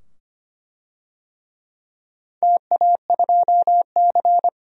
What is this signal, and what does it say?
Morse code, heard as a keyed tone, about 25 wpm, 700 Hz. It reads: TA2C
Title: DXCC-100 Call Signs - 4 Characters - 25wpm